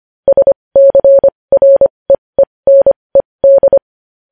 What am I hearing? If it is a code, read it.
Morse code, heard as a keyed tone, about 25 wpm, spaced as for 20 wpm, 550 Hz. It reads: SCREENED
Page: news headlines in morse code at 20 wpm